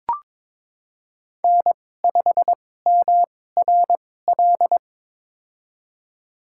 Morse code, heard as a keyed tone, about 22 wpm, spaced as for 16 wpm, 700 Hz. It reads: N5MRL
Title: USA Call Signs - 5 Characters - 2x Character Spacing - 22wpm